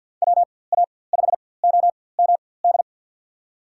Morse code, heard as a keyed tone, about 52 wpm, spaced as for 23 wpm, 700 Hz. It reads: WA5OMD
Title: USA Call Signs - 4x Character Spacing - 50wpm